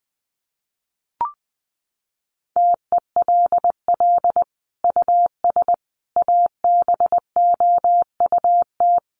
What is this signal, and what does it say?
Morse code, heard as a keyed tone, about 20 wpm, 700 Hz. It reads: TELL US ABOUT
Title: Sentences from Top 1400 Words 20wpm